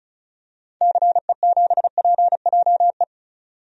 Morse code, heard as a keyed tone, about 36 wpm, 700 Hz. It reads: CE7PJE
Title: DXCC-100 Call Signs - 6 Characters - 35wpm